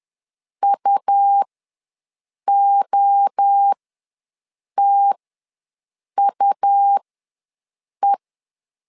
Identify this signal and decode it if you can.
Morse code, heard as a keyed tone, about 10 wpm, spaced as for 6 wpm, 800 Hz. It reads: UOTUE
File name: LEZIONI / 3 MWRO / 3 MWRO ws 10wpm 2310